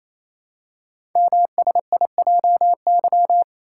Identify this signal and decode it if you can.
Morse code, heard as a keyed tone, about 27 wpm, 700 Hz. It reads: MSIJY
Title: Code Groups - 5 Letters - 28wpm